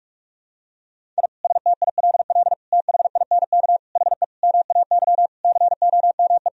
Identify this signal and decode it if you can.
Morse code, heard as a keyed tone, about 46 wpm, 700 Hz. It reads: I STILL THINK HE MAY COME
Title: Sentences from Top 1000 Words - Review Part 1 - 45wpm